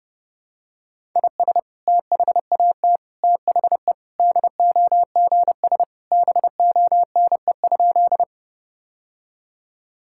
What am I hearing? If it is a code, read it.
Morse code, heard as a keyed tone, about 30 wpm, 700 Hz. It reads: IS THAT THE DOGS BONE?
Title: Sentences from Top 1600 Words 30wpm